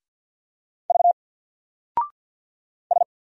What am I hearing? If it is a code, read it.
Morse code, heard as a keyed tone, about 50 wpm, 700 Hz. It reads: V S
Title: Single Character (Letter:Number:Punctuation:Prosign) 50wpm